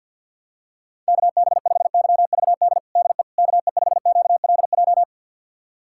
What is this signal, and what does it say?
Morse code, heard as a keyed tone, about 49 wpm, 700 Hz. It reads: KB5YVD DE KE5QLJ
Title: USA Callsign DE Callsign - 6 Characters - 50wpm